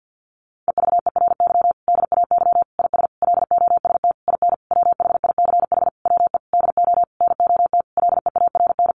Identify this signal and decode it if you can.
Morse code, heard as a keyed tone, about 50 wpm, 700 Hz. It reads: EVERY DAY IS LOST IN WHICH WE DO NOT LEARN
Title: Sentences from Top 1400 Words 50wpm